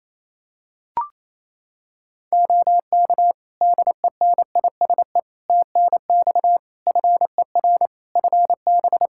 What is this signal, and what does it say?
Morse code, heard as a keyed tone, about 27 wpm, 700 Hz. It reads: OK DENISE TNX FER FB
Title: K1USN SST QSO Practice 28wpm